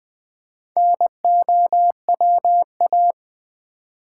Morse code, heard as a keyed tone, about 20 wpm, 700 Hz. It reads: NOWA
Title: Build A Word - 2 Letters At A Time - 8-Letter Words - 20wpm